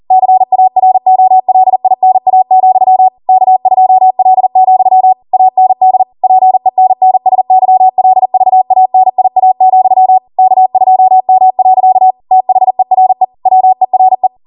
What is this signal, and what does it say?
Morse code, heard as a keyed tone, about 40 wpm, 750 Hz. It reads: CAROLINA, K2L, AND PENNSYLVANIA, K2M. THERE WERE